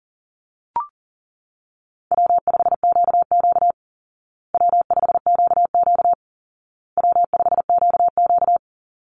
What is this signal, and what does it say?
Morse code, heard as a keyed tone, about 40 wpm, 700 Hz. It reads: W5QQ W5QQ W5QQ
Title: Call Signs Repeated 2x - 40wpm